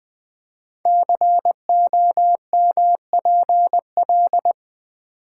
Morse code, heard as a keyed tone, about 20 wpm, 700 Hz. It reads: COMPL
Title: Build A Word - 5 Letters At A Time - 10-Letter Words - 20wpm